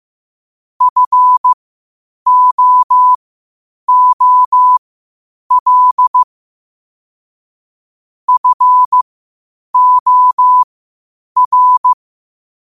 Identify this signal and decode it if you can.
Morse code, heard as a keyed tone, about 15 wpm, spaced as for 8 wpm, 1000 Hz. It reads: FOOL FOR
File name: qotdfiles / 8wpm